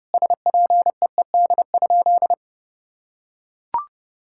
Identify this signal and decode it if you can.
Morse code, heard as a keyed tone, about 30 wpm, 700 Hz. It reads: SPEED?
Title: Sentences from Top 1000 Words - Review Part 5 - 30wpm